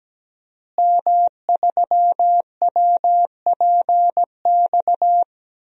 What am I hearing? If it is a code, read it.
Morse code, heard as a keyed tone, about 17 wpm, 700 Hz. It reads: M3WPX
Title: DXCC-100 Call Signs - 5 Characters - Repeated 3x - 17wpm